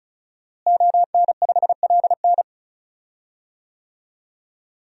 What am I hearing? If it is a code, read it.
Morse code, heard as a keyed tone, about 36 wpm, 700 Hz. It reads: ON5LN
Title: DXCC-100 Call Signs - 5 Characters - Repeated 2x - 35wpm